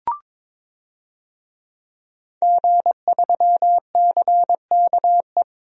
Morse code, heard as a keyed tone, about 22 wpm, 700 Hz. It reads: G3CKE
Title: DXCC-100 Call Signs - 5 Characters - Repeated 2x - 22wpm